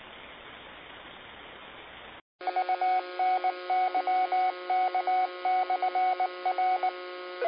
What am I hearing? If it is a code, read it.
Morse code, heard as a keyed tone, about 19 wpm, 740 Hz, with noise. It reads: UNYK/R
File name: qsorec SimplexLogic 19-10-58